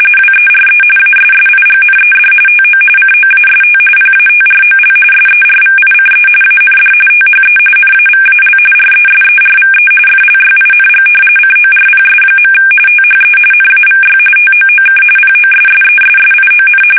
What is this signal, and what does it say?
RTTY, about 45.45 baud. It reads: WERE ASSIGNED TO COAST PATROL SERVICE AND THE NAVAL RESERVE FLYING CORPS. THE NAVY TOOK POSSESSION
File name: fsk850